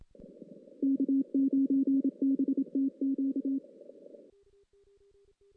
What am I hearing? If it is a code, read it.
Morse code, heard as a keyed tone, about 27 wpm, 270 Hz. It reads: K9BTQ
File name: 2016 JAN CW NAQP